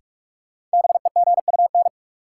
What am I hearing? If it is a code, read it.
Morse code, heard as a keyed tone, about 46 wpm, 700 Hz. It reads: BEGUN
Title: Sentences from Top 1000 Words - Review Part 6 - New Word Drill - 45wpm